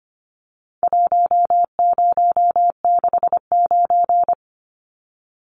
Morse code, heard as a keyed tone, about 25 wpm, 700 Hz. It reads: 1069
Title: Numbers - Contest Style - 4-Digit 25wpm